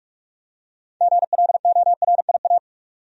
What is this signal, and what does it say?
Morse code, heard as a keyed tone, about 46 wpm, 700 Hz. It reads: GLORIA
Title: USA Common Names - 45wpm